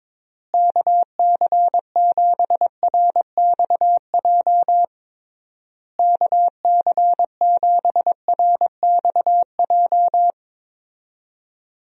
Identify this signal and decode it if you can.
Morse code, heard as a keyed tone, about 22 wpm, 700 Hz. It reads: KC7RXJ KC7RXJ
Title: USA Call Signs - 6 Characters - Repeated 3x - 22wpm